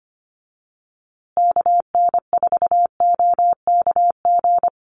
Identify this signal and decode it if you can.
Morse code, heard as a keyed tone, about 25 wpm, 700 Hz. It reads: KN4OKG
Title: USA Call Signs Repeated 3x 25wpm